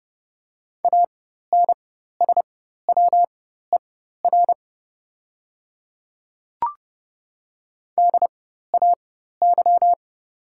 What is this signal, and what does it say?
Morse code, heard as a keyed tone, about 31 wpm, spaced as for 14 wpm, 700 Hz. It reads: ANSWER DAY
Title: 4x Character Spacing - Top 300 Words - 30wpm